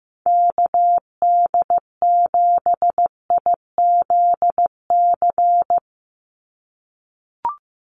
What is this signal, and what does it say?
Morse code, heard as a keyed tone, about 15 wpm, 700 Hz. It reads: KD7IZC
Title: USA Call Signs 15wpm